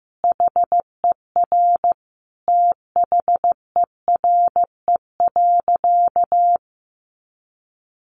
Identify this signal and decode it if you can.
Morse code, heard as a keyed tone, about 15 wpm, 700 Hz. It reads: HER THERE.
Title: Sentences from Top 1500 Words 15wpm